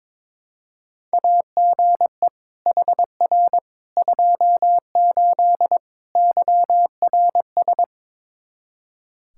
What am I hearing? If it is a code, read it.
Morse code, heard as a keyed tone, about 21 wpm, 700 Hz. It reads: AGE HR 28 YRS
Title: Practice QSO - Exchange 3 - More Info - 22wpm